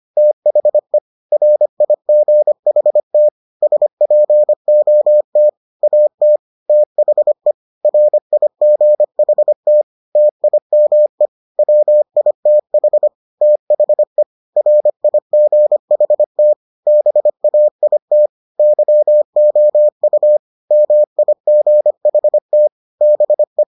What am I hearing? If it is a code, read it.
Morse code, heard as a keyed tone, about 25 wpm, 600 Hz. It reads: THE RIGHT SPOT AT THE RIGHT TIME WITH THE RIGHT BAIT YOU MIGHT BE